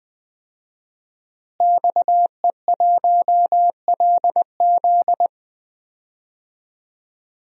Morse code, heard as a keyed tone, about 20 wpm, 700 Hz. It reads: XE1LZ
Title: DXCC-100 Call Signs - 5 Characters - Repeated 2x - 20wpm